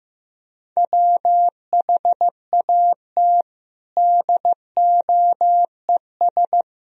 Morse code, heard as a keyed tone, about 15 wpm, 700 Hz. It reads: WHAT DOES
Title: Sentences from Top 1000 Words - Review Part 4 - 15wpm